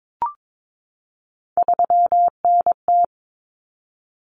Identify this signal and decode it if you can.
Morse code, heard as a keyed tone, about 22 wpm, 700 Hz. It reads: 3NT
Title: Common Cut Numbers - 3-Digit 22wpm